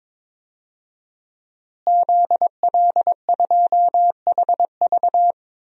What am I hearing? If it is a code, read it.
Morse code, heard as a keyed tone, about 22 wpm, 700 Hz. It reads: ZL2HV
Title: DXCC-100 Call Signs - 5 Characters - Repeated 3x - 22wpm